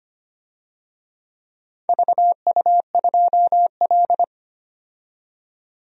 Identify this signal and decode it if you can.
Morse code, heard as a keyed tone, about 25 wpm, 700 Hz. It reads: VU2L
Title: DXCC-100 Call Signs - 4 Characters - 25wpm